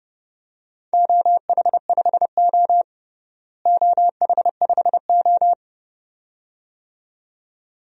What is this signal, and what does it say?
Morse code, heard as a keyed tone, about 30 wpm, 700 Hz. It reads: OH5O OH5O
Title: DXCC-100 Call Signs - 4 Characters - Repeated 2x - 30wpm